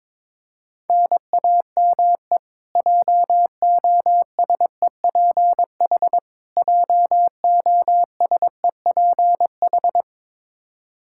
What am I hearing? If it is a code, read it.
Morse code, heard as a keyed tone, about 22 wpm, 700 Hz. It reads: NAME JOSEPH JOSEPH
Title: K1USN SST QSO Practice 22wpm